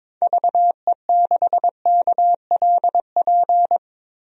Morse code, heard as a keyed tone, about 22 wpm, 700 Hz. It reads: VE6KLP
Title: DXCC-100 Call Signs - 6 Characters - Repeated 2x - 22wpm